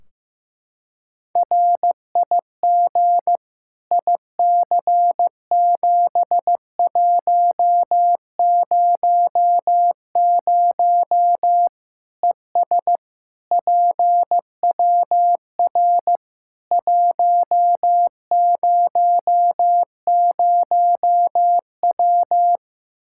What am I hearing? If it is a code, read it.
Morse code, heard as a keyed tone, about 15 wpm, 700 Hz. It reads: RIG IC7100 ES PWR 100W
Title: Practice QSO - Exchange 2 - Info - 15wpm